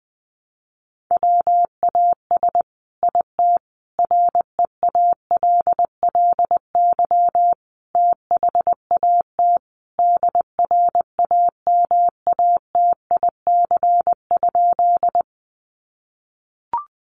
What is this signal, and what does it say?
Morse code, heard as a keyed tone, about 20 wpm, 700 Hz. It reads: WAS IT REALLY THAT DRAMATIC?
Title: Sentences from Top 1300 Words 20wpm